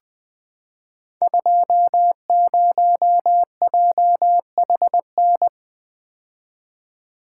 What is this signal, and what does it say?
Morse code, heard as a keyed tone, about 20 wpm, 700 Hz. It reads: 20JHN
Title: Code Groups - 5 Letter:Numbers - 20wpm